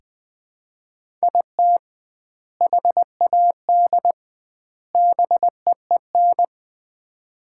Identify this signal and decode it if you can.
Morse code, heard as a keyed tone, about 21 wpm, 700 Hz. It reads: IT HAD BEEN
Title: 2x Word-Spacing - Sets of 3 Words - 20wpm